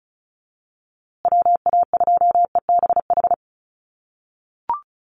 Morse code, heard as a keyed tone, about 34 wpm, 700 Hz. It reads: WA2EBH
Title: USA Call Signs Repeated 2x 35wpm